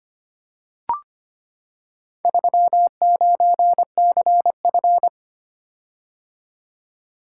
Morse code, heard as a keyed tone, about 25 wpm, 690 Hz. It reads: 39CF
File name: practice / 25wpm / Four Letter-Number Combinations - 25wpm